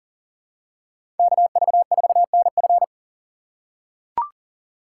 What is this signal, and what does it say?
Morse code, heard as a keyed tone, about 40 wpm, 700 Hz. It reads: KV4NF
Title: USA Call Signs - 5 Characters - 2x Character Spacing - 40wpm